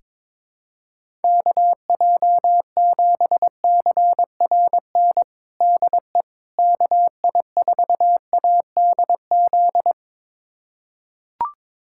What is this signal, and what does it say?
Morse code, heard as a keyed tone, about 22 wpm, 700 Hz. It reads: KJ7CRN DE KI4ADZ
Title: USA Callsign DE Callsign - 6 Characters - 22wpm